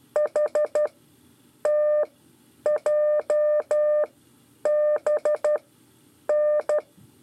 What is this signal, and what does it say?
Morse code, heard as a keyed tone, about 12 wpm, spaced as for 8 wpm, 600 Hz. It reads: HTJBN
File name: 5 LEZIONE RIPASSO 1